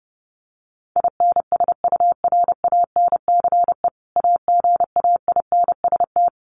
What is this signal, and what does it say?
Morse code, heard as a keyed tone, about 30 wpm, 700 Hz. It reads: INSURANCE AGAINST
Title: Sentences from Top 1100 Words 30wpm